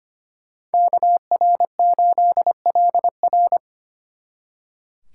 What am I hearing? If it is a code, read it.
Morse code, heard as a keyed tone, about 30 wpm, 700 Hz. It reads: KR8LR
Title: Speed Racing - USA 2x2 Call Signs 25wpm